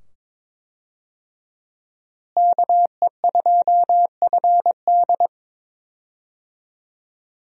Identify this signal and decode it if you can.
Morse code, heard as a keyed tone, about 22 wpm, 700 Hz. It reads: KE2FD
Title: USA Call Signs - 5 Characters - Repeated 2x - 22wpm